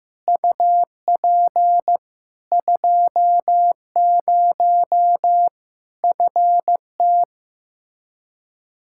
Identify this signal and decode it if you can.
Morse code, heard as a keyed tone, about 15 wpm, 700 Hz. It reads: UP 20 FT
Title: K1USN SST QSO Practice 15wpm